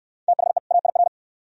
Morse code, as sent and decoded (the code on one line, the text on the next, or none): - .... / .- .. .-.
THE AIR